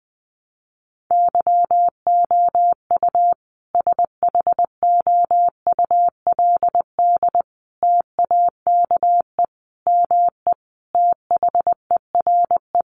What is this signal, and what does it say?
Morse code, heard as a keyed tone, about 20 wpm, 700 Hz. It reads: YOU SHOULD TAKE ME THERE
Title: Sentences from Top 300 Words 20wpm